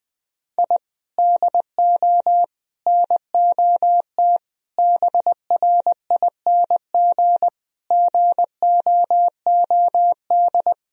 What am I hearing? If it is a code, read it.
Morse code, heard as a keyed tone, about 20 wpm, 700 Hz. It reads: I DO NOT BRING GOOD
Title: Sentences from Top 1000 Words - Review Part 3 - 20wpm